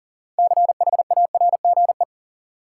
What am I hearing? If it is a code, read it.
Morse code, heard as a keyed tone, about 40 wpm, 700 Hz. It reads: CHARGE
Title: Six Letter Words - 40wpm